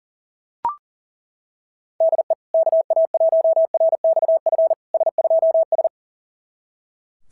Morse code, heard as a keyed tone, about 39 wpm, 690 Hz. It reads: DE KA1RXF S2S
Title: SOTA QSO Practice 40wpm